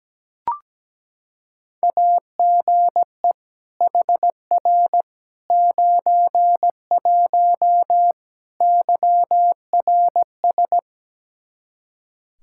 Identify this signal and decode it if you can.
Morse code, heard as a keyed tone, about 17 wpm, 700 Hz. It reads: AGE HR 91 YRS
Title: K1USN SST QSO Practice 17wpm